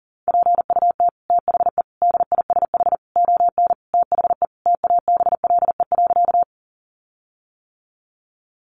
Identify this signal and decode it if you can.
Morse code, heard as a keyed tone, about 40 wpm, 700 Hz. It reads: PUT THE DISH ON THE TABLE.
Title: Sentences from Top 1300 Words 40wpm